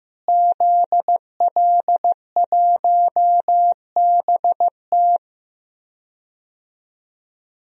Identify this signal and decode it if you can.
Morse code, heard as a keyed tone, about 15 wpm, 700 Hz. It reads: ZL1BT
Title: DXCC-100 Call Signs - 5 Characters - Repeated 2x - 15wpm